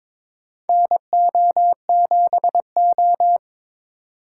Morse code, heard as a keyed tone, about 22 wpm, 700 Hz. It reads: NO7O
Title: USA Call Signs - 4 Characters - 2x Character Spacing - 22wpm